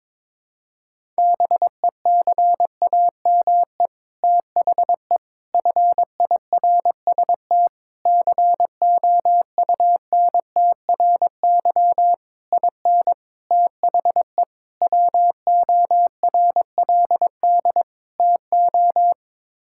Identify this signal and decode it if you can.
Morse code, heard as a keyed tone, about 22 wpm, 700 Hz. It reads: BECAME THE FIRST COUNTRY IN THE WORLD TO
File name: practice / 22wpm / Sets of 8 Words - 22wpm